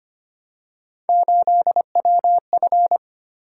Morse code, heard as a keyed tone, about 25 wpm, 700 Hz. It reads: 8WF
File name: Three Letter-Number Combinations - 25wpm